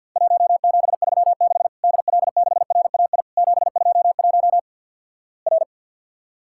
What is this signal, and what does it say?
Morse code, heard as a keyed tone, about 50 wpm, 700 Hz. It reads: 1736 DL6RAI 721 R
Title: Worked All Europe DX Contest - QTC Exchange with Repeats - 50wpm